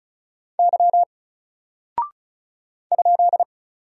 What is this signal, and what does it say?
Morse code, heard as a keyed tone, about 36 wpm, 700 Hz. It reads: Y ?
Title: Single Character (Letter:Number:Punctuation:Prosign) 35wpm